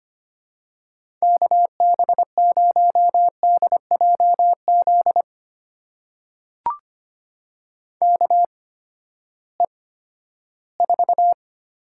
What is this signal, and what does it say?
Morse code, heard as a keyed tone, about 25 wpm, spaced as for 14 wpm, 700 Hz. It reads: KB0DJZ KE4
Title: USA Call Signs - 6 Characters - 8x Character Spacing - 25wpm